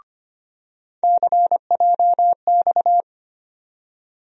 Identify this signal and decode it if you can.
Morse code, heard as a keyed tone, about 25 wpm, 700 Hz. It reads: CJX
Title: USA Call Sign Postfixes - 3 Letters 25wpm